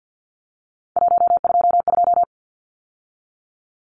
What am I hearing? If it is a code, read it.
Morse code, heard as a keyed tone, about 50 wpm, 700 Hz. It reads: 122
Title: Numbers - 3-Digit 50wpm